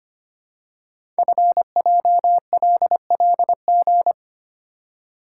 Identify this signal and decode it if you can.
Morse code, heard as a keyed tone, about 25 wpm, 700 Hz. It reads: FJLLG
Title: Code Groups - 5 Letters - 25wpm